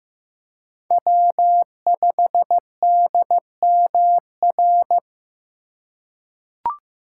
Morse code, heard as a keyed tone, about 15 wpm, 700 Hz. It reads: W5DMR
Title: USA Call Signs - 5 Characters - 3x Character Spacing - 15wpm